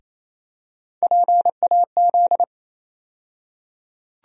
Morse code, heard as a keyed tone, about 28 wpm, 700 Hz. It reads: PAZ
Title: USA Call Sign Postfixes - 3 Letters 28wpm